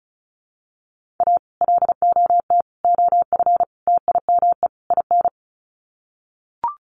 Morse code, heard as a keyed tone, about 34 wpm, 700 Hz. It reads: A LOT OF TIME IN
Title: Sets of 5 Words Encore2 35wpm